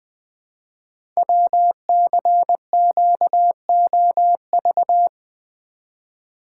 Morse code, heard as a keyed tone, about 20 wpm, 700 Hz. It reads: WCQOV
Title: Code Groups - 5 Letters - 20wpm